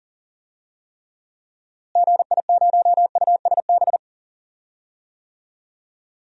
Morse code, heard as a keyed tone, about 37 wpm, 700 Hz. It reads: GI0USB
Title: DXCC-100 Call Signs - 6 Characters - Repeated 3x - 40wpm